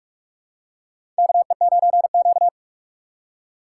Morse code, heard as a keyed tone, about 45 wpm, 700 Hz. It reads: KE9Q
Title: USA Call Signs - 4 Characters - 2x Character Spacing - 45wpm